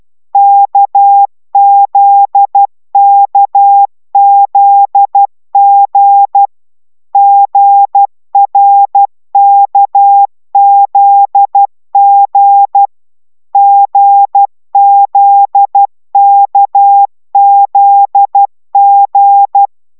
Morse code, heard as a keyed tone, about 12 wpm, 800 Hz. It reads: KZKZG GRKZG GZKZG
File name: taviro tanulas / cw 69 15 10